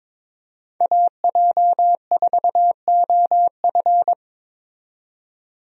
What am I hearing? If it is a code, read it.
Morse code, heard as a keyed tone, about 22 wpm, 700 Hz. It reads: AJ4OF
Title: USA Call Signs - 5 Characters - 22wpm